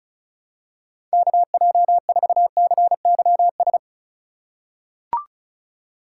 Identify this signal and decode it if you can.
Morse code, heard as a keyed tone, about 34 wpm, 700 Hz. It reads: KJ4CYS
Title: USA Call Signs - 2x Character Spacing - 35wpm